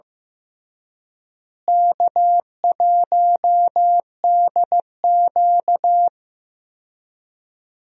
Morse code, heard as a keyed tone, about 15 wpm, 700 Hz. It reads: K1DQ
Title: USA Call Signs - 4 Characters - Repeated 3x - 15wpm